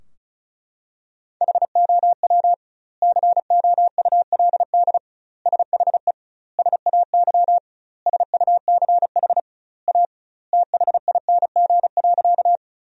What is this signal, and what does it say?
Morse code, heard as a keyed tone, about 34 wpm, 700 Hz. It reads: HOW COULD SHE SAY SUCH A THING.
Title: 2x Word-Spacing - Sentences from Top 200 Words - 35wpm